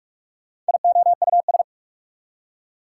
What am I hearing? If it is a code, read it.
Morse code, heard as a keyed tone, about 43 wpm, 690 Hz. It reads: IOUS
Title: Prefixes and Suffixes 45wpm